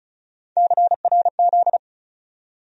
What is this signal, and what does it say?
Morse code, heard as a keyed tone, about 36 wpm, 700 Hz. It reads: CRZ
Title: USA Call Sign Postfixes - 3 Letters 35wpm